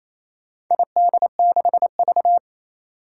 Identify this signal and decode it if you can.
Morse code, heard as a keyed tone, about 28 wpm, 700 Hz. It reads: ID6V
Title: Four Challenging Letter-Number Combinations 28wpm